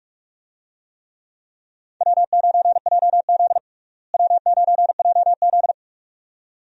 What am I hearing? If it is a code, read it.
Morse code, heard as a keyed tone, about 43 wpm, 700 Hz. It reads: W9JZ W9JZ
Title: USA Call Signs - 4 Characters - Repeated 2x - 45wpm